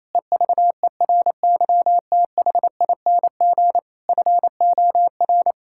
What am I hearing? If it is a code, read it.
Morse code, heard as a keyed tone, about 28 wpm, 700 Hz. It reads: EVERYTHING FOR
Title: Sentences from Top 1000 Words - Review Part 6 - 28wpm